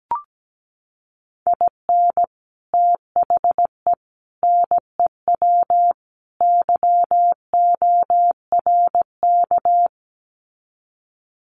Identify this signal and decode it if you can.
Morse code, heard as a keyed tone, about 17 wpm, 700 Hz. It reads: IN THE NEW YORK
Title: Sets of 4 Words 17wpm